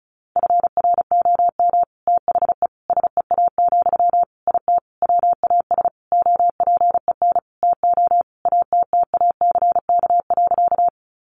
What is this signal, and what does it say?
Morse code, heard as a keyed tone, about 34 wpm, 700 Hz. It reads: FROM THE SEA, IT WAS OPEN TO ATTACK.
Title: Sentences from Top 1500 Words 35wpm